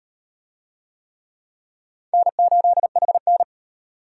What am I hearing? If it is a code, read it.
Morse code, heard as a keyed tone, about 38 wpm, 700 Hz. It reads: N8HN